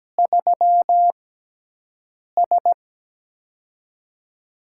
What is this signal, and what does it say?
Morse code, heard as a keyed tone, about 17 wpm, spaced as for 6 wpm, 700 Hz. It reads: 3S
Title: USA Call Signs - 6x Character Spacing - 17wpm